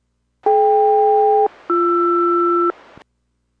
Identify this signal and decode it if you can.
Aircraft SELCAL, heard as two pairs of tones. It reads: DK-BR